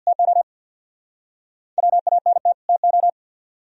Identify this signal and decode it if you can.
Morse code, heard as a keyed tone, about 62 wpm, 700 Hz. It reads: TO WANT TO
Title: Speed-Racing - Sets of 2 Words 50wpm